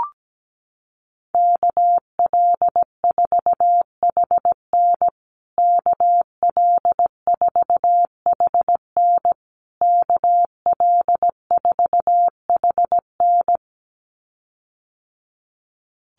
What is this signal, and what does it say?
Morse code, heard as a keyed tone, about 17 wpm, 700 Hz. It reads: KL4HN KL4HN KL4HN
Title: USA Call Signs Repeated 3x 17wpm